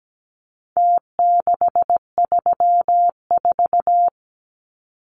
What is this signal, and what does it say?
Morse code, heard as a keyed tone, about 17 wpm, 700 Hz. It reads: T634
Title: Numbers - Contest Style - 4-Digit 17wpm